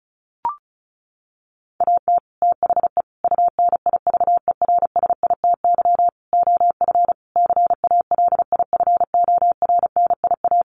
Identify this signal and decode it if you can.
Morse code, heard as a keyed tone, about 34 wpm, 700 Hz. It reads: AT THE UNIVERSITY OF CALIFORNIA
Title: Sets of 5 Words Encore2 35wpm